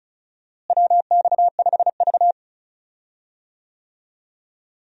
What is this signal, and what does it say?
Morse code, heard as a keyed tone, about 36 wpm, 700 Hz. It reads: WX5V
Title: USA Call Signs - 4 Characters - Repeated 2x - 35wpm